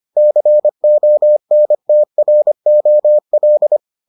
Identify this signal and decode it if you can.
Morse code, heard as a keyed tone, about 25 wpm, 600 Hz. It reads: CONTROL